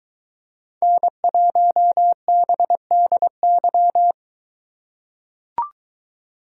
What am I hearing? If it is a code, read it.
Morse code, heard as a keyed tone, about 23 wpm, 700 Hz. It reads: N1BDY